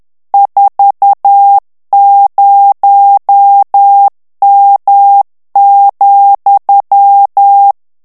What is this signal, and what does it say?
Morse code, heard as a keyed tone, about 10 wpm, 800 Hz. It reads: 40M,
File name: morse mp3 / ITALIEN texte clair 1 vitesse 10 mots minute